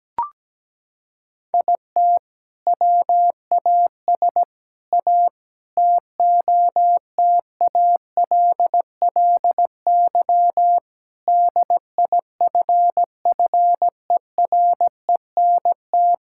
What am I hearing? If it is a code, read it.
Morse code, heard as a keyed tone, about 17 wpm, 700 Hz. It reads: IT WAS A TOTALLY DIFFERENT
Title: Sentences from Top 1600 Words 17wpm